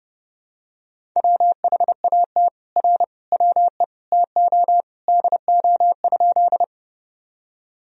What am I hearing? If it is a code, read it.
Morse code, heard as a keyed tone, about 32 wpm, 700 Hz. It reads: WHAT R WE TO DO?
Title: Sentences from Top 100 Words and Abbreviations 30wpm